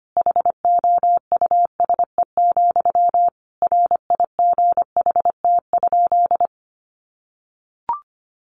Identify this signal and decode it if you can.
Morse code, heard as a keyed tone, about 25 wpm, 700 Hz. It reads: HOUSE, RIGHT?
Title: Sentences from Top 1200 Words 25wpm